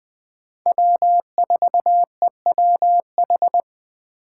Morse code, heard as a keyed tone, about 20 wpm, 700 Hz. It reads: W4EWH
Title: Speed Racing - USA 1x3 Call Signs 17wpm